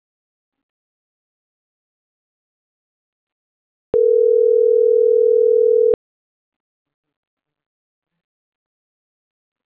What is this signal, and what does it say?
Ringback tone (North America), 440 + 480 Hz.